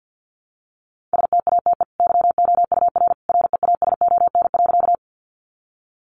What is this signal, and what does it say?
Morse code, heard as a keyed tone, about 49 wpm, 700 Hz. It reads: STATE YOUR REASON.
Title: Sentences from Top 700 Words 50wpm